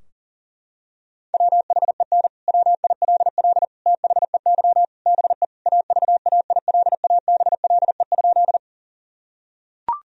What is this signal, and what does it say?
Morse code, heard as a keyed tone, about 40 wpm, 700 Hz. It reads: WHEN WILL THEY BE AVAILABLE?
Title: Sentences from Top 1000 Words - Review Part 3 - 40wpm